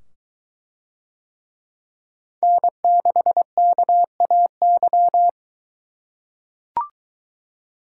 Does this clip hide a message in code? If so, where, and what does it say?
Yes: Morse code from 2.4 to 5.3 s, heard as a keyed tone, about 21 wpm, 700 Hz. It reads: N6KAY